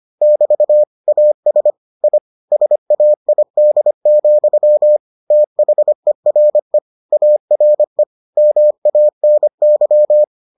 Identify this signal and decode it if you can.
Morse code, heard as a keyed tone, about 25 wpm, 600 Hz. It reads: <BT> AS I SAID, THERE ARE MANY